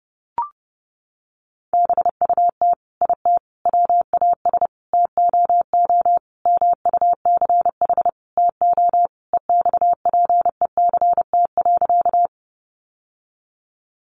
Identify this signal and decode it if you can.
Morse code, heard as a keyed tone, about 30 wpm, 700 Hz. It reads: BUT IT WAS TOO MUCH TO EXPECT.
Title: Sentences from Top 1100 Words 30wpm